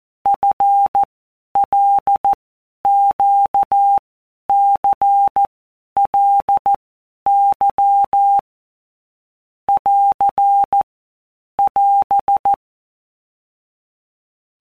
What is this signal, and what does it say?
Morse code, heard as a keyed tone, about 14 wpm, 800 Hz. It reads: F L Q C L Y <AR> <AS>
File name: Basic Letter Groups By WPM's / Groups@13 / LFCYQ@13